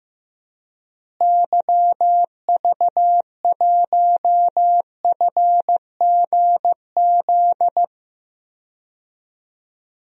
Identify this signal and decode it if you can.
Morse code, heard as a keyed tone, about 15 wpm, 700 Hz. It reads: YV1FGZ
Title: DXCC-100 Call Signs - 6 Characters - Repeated 3x - 15wpm